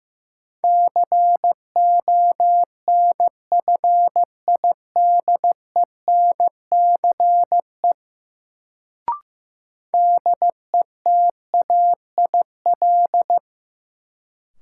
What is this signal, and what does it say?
Morse code, heard as a keyed tone, about 15 wpm, 700 Hz. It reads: CONFIDENCE DETAIL
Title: Sentences from Top 1500 Words - New Word Drill Alt - 15wpm